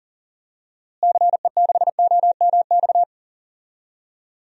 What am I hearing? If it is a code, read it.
Morse code, heard as a keyed tone, about 40 wpm, 700 Hz. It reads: CE6OMX
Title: DXCC-100 Call Signs - 6 Characters - Repeated 2x - 40wpm